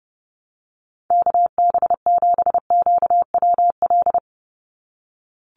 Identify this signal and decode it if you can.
Morse code, heard as a keyed tone, about 30 wpm, 700 Hz. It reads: KB7QWL
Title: USA Call Signs 30wpm